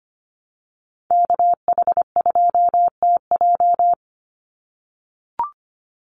Morse code, heard as a keyed tone, about 25 wpm, 700 Hz. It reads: KH2TJ
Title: NA SOTA Activator Callsigns 25wpm